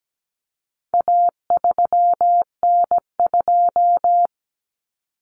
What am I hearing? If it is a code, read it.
Morse code, heard as a keyed tone, about 17 wpm, 700 Hz. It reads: A3N2
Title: Common Cut Numbers - 4-Digit 17wpm